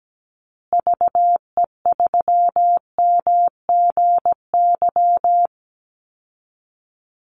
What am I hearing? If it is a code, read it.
Morse code, heard as a keyed tone, about 17 wpm, 700 Hz. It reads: VE3MGY
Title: Call Signs Repeated 3x - 17wpm